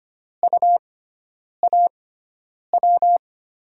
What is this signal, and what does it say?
Morse code, heard as a keyed tone, about 25 wpm, spaced as for 9 wpm, 700 Hz. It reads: UAW
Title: USA Call Signs - 5 Characters - 6x Character Spacing - 25wpm